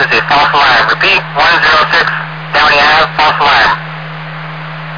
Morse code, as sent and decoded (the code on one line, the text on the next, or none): none